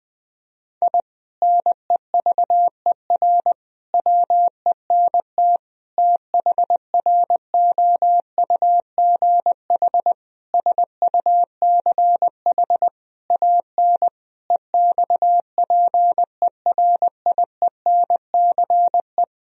Morse code, heard as a keyed tone, about 20 wpm, 700 Hz. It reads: I NEVER WENT THROUGH SUCH AN EXPERIENCE
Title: Sentences from Top 1000 Words - Review Part 3 - 20wpm